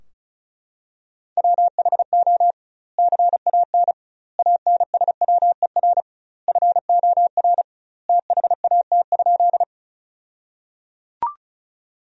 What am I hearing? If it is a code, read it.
Morse code, heard as a keyed tone, about 36 wpm, 700 Hz. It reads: WHO CAN ANSWER FOR THAT?
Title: 2x Word-Spacing - Sentences from Top 300 Words - 35wpm